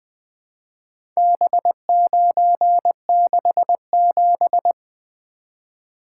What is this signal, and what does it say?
Morse code, heard as a keyed tone, about 20 wpm, 700 Hz. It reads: B967
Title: Four Letter-Number Combinations - 20wpm